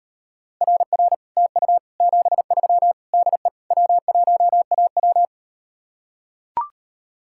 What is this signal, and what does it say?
Morse code, heard as a keyed tone, about 38 wpm, 700 Hz. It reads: RR TU 73 DE W1AW